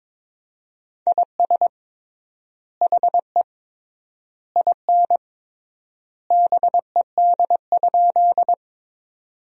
Morse code, heard as a keyed tone, about 22 wpm, 700 Hz. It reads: IS HE IN BED?
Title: 3x Word-Spacing - Sentences from Top 500 Words - 22wpm